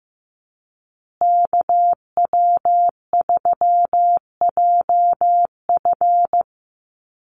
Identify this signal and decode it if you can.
Morse code, heard as a keyed tone, about 15 wpm, 700 Hz. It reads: KW3JF
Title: USA Call Signs Repeated 3x 15wpm